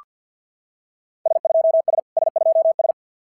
Morse code, heard as a keyed tone, about 48 wpm, 690 Hz. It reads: S2S S2S
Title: SOTA QSO Practice 50wpm